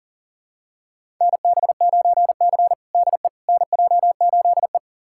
Morse code, heard as a keyed tone, about 40 wpm, 700 Hz. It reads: NB9C DE NJ8E
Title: USA Callsign DE Callsign - 4 Characters - 40wpm